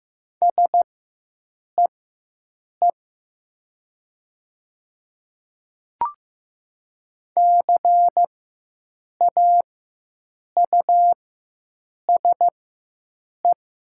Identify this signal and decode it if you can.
Morse code, heard as a keyed tone, about 15 wpm, spaced as for 7 wpm, 700 Hz. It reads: SEE CAUSE
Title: 4x Character Spacing - Top 200 Words - 15wpm